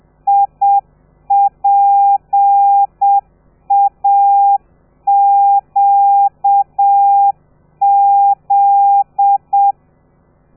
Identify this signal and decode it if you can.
Morse code, heard as a keyed tone, about 7 wpm, 800 Hz. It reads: IPAQZ
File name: telegrafia / i2tfj / Lez13 30